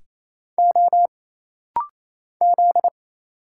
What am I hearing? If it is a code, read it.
Morse code, heard as a keyed tone, about 27 wpm, 700 Hz. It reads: O Z
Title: Single Letters - Rapid-Fire 28wpm